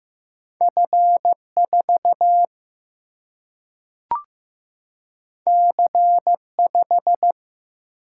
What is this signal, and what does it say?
Morse code, heard as a keyed tone, about 15 wpm, 700 Hz. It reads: F 4 C 5
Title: Single-Letter Single-Number 15wpm